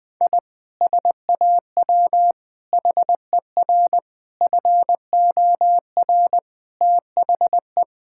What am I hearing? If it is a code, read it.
Morse code, heard as a keyed tone, about 20 wpm, 700 Hz. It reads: I SAW HER FOR THE
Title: Sentences with 4-Word Phrases - 20wpm